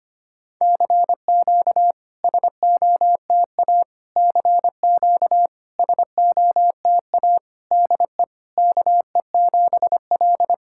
Morse code, heard as a keyed tone, about 25 wpm, 690 Hz. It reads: CQ SOTA CQ SOTA DE KE7L
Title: SOTA QSO Practice 25wpm